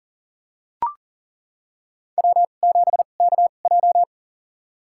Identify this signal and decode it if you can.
Morse code, heard as a keyed tone, about 43 wpm, spaced as for 30 wpm, 690 Hz. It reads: W7KJ
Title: USA Call Signs - 4 Characters - 2x Character Spacing - 40wpm